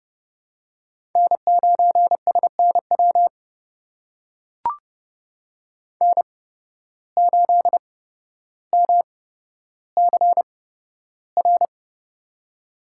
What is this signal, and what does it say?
Morse code, heard as a keyed tone, about 31 wpm, spaced as for 13 wpm, 700 Hz. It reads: N9SNW N8MCR
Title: USA Call Signs - 5 Characters - 8x Character Spacing - 30wpm